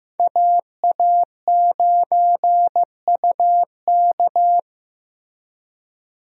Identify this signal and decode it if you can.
Morse code, heard as a keyed tone, about 15 wpm, 700 Hz. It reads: AA9UK
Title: USA Call Signs - 5 Characters - 15wpm